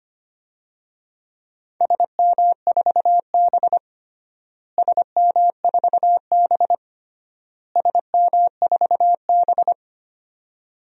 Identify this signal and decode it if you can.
Morse code, heard as a keyed tone, about 25 wpm, 700 Hz. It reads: SM4B SM4B SM4B
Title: DXCC-100 Call Signs - 4 Characters - Repeated 3x - 25wpm